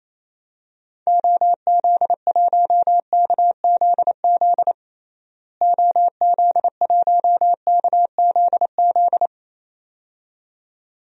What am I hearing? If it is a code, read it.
Morse code, heard as a keyed tone, about 28 wpm, 700 Hz. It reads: OZ1KZZ OZ1KZZ
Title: DXCC-100 Call Signs - 6 Characters - Repeated 2x - 28wpm